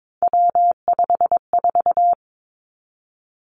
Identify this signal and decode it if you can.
Morse code, heard as a keyed tone, about 22 wpm, 700 Hz. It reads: W54
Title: Three Challenging Letter-Number Combinations 22wpm